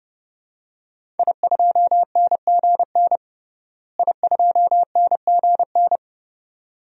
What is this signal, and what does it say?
Morse code, heard as a keyed tone, about 30 wpm, 700 Hz. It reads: I2NGN I2NGN
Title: DXCC-100 Call Signs - 5 Characters - Repeated 2x - 30wpm